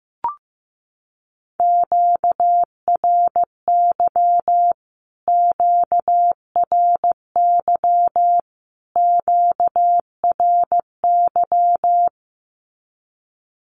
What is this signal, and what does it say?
Morse code, heard as a keyed tone, about 15 wpm, 700 Hz. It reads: QRY QRY QRY
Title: Q-Codes Repeated 3x 15wpm